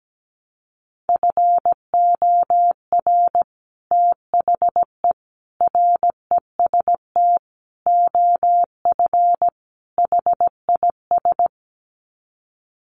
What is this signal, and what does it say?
Morse code, heard as a keyed tone, about 17 wpm, 700 Hz. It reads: FOR THE REST OF HIS
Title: Sets of 5 Words 17wpm